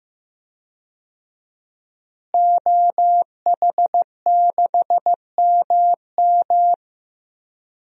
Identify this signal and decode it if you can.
Morse code, heard as a keyed tone, about 15 wpm, 700 Hz. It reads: OH6MM
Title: DXCC-100 Call Signs - 5 Characters - Repeated 3x - 15wpm